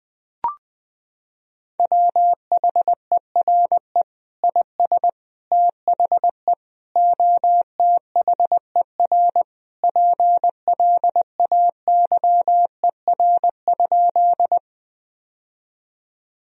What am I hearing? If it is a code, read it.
Morse code, heard as a keyed tone, about 21 wpm, 700 Hz. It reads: WHERE IS THE OTHER PLAYER?
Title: Sentences from Top 1000 Words - Review Part 4 - 20wpm